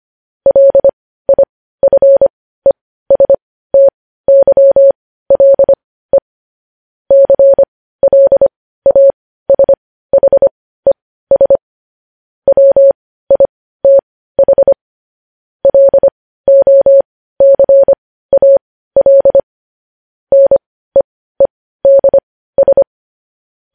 Morse code, heard as a keyed tone, about 25 wpm, spaced as for 15 wpm, 550 Hz. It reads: LIFESTYLE CLASHES WITH LOCAL NEEDS
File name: podcast / News headlines-1x-25x15